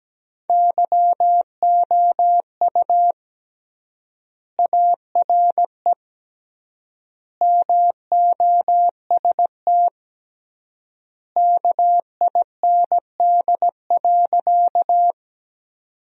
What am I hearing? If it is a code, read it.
Morse code, heard as a keyed tone, about 17 wpm, 700 Hz. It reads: YOU ARE MOST KIND.
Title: 3x Word-Spacing - Sentences from Top 200 Words - 17wpm